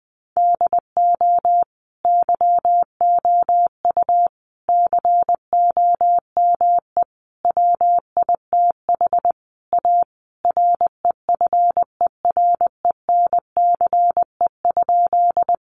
Morse code, heard as a keyed tone, about 20 wpm, 700 Hz. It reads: DO YOU COME WITH A REFERENCE?
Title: Sentences from Top 1200 Words 20wpm